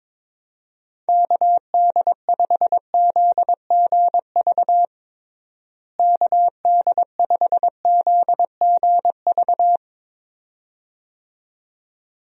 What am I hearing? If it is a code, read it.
Morse code, heard as a keyed tone, about 22 wpm, 700 Hz. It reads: KD5ZGV KD5ZGV
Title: USA Call Signs - 6 Characters - Repeated 2x - 22wpm